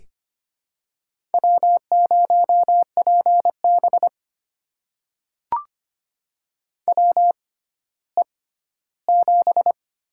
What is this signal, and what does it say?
Morse code, heard as a keyed tone, about 25 wpm, spaced as for 14 wpm, 700 Hz. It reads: W0PB WE7
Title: USA Call Signs - 4 Characters - 6x Character Spacing - 25wpm